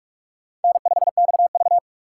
Morse code, heard as a keyed tone, about 45 wpm, 700 Hz. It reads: N5XV